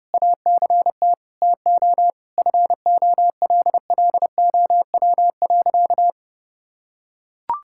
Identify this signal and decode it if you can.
Morse code, heard as a keyed tone, about 32 wpm, 700 Hz. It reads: ACT TO FOLLOW.
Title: Sentences from Top 1000 Words - Review Part 6 - 30wpm